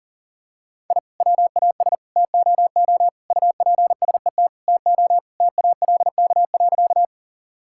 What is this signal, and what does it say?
Morse code, heard as a keyed tone, about 40 wpm, 700 Hz. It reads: I WAS TOO UPSET TO TALK.
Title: Sentences from Top 1000 Words - Review Part 5 - 40wpm